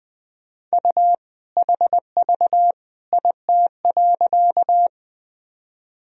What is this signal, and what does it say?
Morse code, heard as a keyed tone, about 19 wpm, 700 Hz. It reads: U HV IT.
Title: Sentences from Top 100 Words and Abbreviations 20wpm